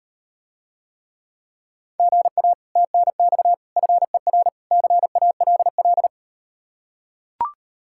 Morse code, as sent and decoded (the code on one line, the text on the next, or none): --. .- / - -. -..- / ..-. .-. / -.-. .- .-.. .-..
GA TNX FER CALL